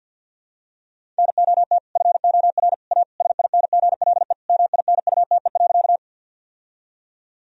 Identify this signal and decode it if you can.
Morse code, heard as a keyed tone, about 50 wpm, 700 Hz. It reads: NOT FOR A SINGLE MINUTE.
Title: Sentences from Top 1000 Words - Review Part 4 - 50wpm